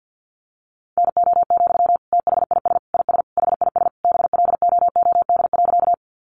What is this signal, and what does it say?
Morse code, heard as a keyed tone, about 50 wpm, 700 Hz. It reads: NO, THIS IS HIS BLOOD.
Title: Sentences from Top 1100 Words 50wpm